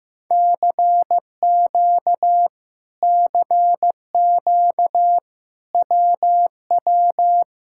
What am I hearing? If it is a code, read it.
Morse code, heard as a keyed tone, about 15 wpm, 700 Hz. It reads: CQ CQ WW
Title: Speed-Racing Common QSO Phrases Encore - 15wpm